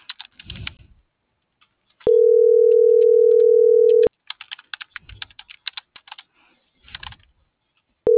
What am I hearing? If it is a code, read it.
Ringback tone (North America), 440 + 480 Hz.